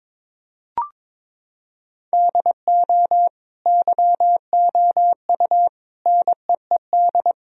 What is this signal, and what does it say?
Morse code, heard as a keyed tone, about 21 wpm, 700 Hz. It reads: DO YOU NEED
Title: Sentences from Top 1000 Words - Review Part 4 - 22wpm